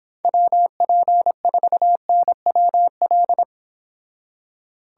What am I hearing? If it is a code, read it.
Morse code, heard as a keyed tone, about 25 wpm, 700 Hz. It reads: WP4NWL